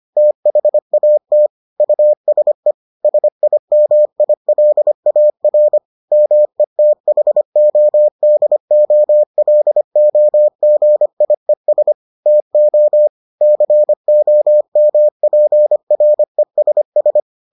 Morse code, heard as a keyed tone, about 25 wpm, 600 Hz. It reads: THAT USE SIMILAR METHODOLOGIES TO COMPRESS